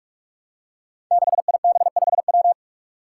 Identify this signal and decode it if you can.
Morse code, heard as a keyed tone, about 44 wpm, 700 Hz. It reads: 6IB5W
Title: Five Challenging Letter-Number Combinations 45wpm